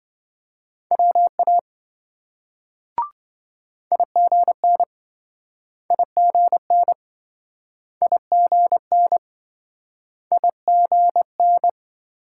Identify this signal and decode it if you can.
Morse code, heard as a keyed tone, about 26 wpm, 700 Hz. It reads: WA IGN IGN IGN IGN